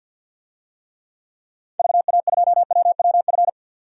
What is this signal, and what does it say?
Morse code, heard as a keyed tone, about 47 wpm, 700 Hz. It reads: VA2WWF
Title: DXCC-100 Call Signs - 6 Characters - 50wpm